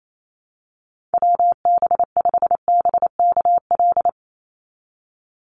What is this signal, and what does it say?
Morse code, heard as a keyed tone, about 28 wpm, 700 Hz. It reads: WB5BKL
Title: NA SOTA Chaser Callsigns 28wpm